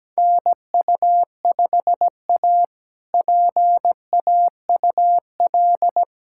Morse code, heard as a keyed tone, about 17 wpm, 700 Hz. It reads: NU5A PAUL
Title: K1USN SST QSO Practice - 17wpm